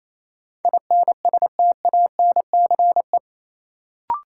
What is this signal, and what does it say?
Morse code, heard as a keyed tone, about 28 wpm, 700 Hz. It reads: INSTANCE
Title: Sentences from Top 800 Words - New Word Drill - 28wpm